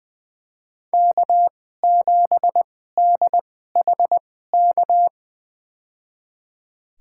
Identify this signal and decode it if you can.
Morse code, heard as a keyed tone, about 21 wpm, spaced as for 15 wpm, 700 Hz. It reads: K7DHK
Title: USA Call Signs - 5 Characters - 2x Character Spacing - 20wpm